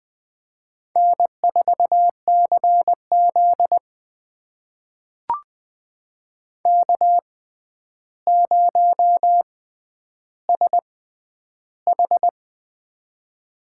Morse code, heard as a keyed tone, about 21 wpm, spaced as for 10 wpm, 700 Hz. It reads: N4CZ K0SH
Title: USA Call Signs - 4 Characters - 6x Character Spacing - 20wpm